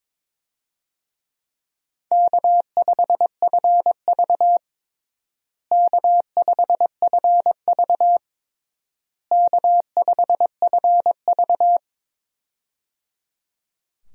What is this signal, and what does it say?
Morse code, heard as a keyed tone, about 22 wpm, 700 Hz. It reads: K5FV K5FV K5FV
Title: USA Call Signs - 4 Characters - Repeated 3x - 22wpm